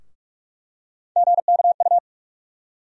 Morse code, heard as a keyed tone, about 44 wpm, 700 Hz. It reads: GKU
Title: USA Call Sign Postfixes - 3 Letters 45wpm